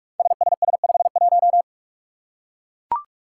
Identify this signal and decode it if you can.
Morse code, heard as a keyed tone, about 44 wpm, 700 Hz. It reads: SSS51
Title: Five Challenging Letter-Number Combinations 45wpm